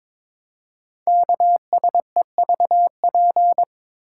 Morse code, heard as a keyed tone, about 23 wpm, 700 Hz. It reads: KSEVP
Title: Code Groups - 5 Letter:Numbers - 22wpm